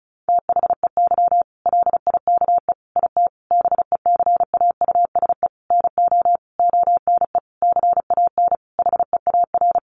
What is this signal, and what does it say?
Morse code, heard as a keyed tone, about 34 wpm, 700 Hz. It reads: THEY LIKE IT BECAUSE NO ONE CAN HEAR
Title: Sentences from Top 600 Words 35wpm